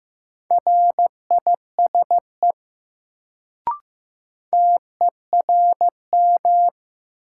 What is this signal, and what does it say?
Morse code, heard as a keyed tone, about 15 wpm, 700 Hz. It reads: RISE TERM
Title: Sentences from Top 900 Words - New Word Drill - 15wpm